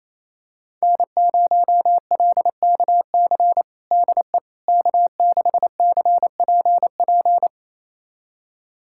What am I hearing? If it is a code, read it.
Morse code, heard as a keyed tone, about 28 wpm, 700 Hz. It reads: N0LKC DE K6CPP
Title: USA Callsign DE Callsign - 5 Characters - 28wpm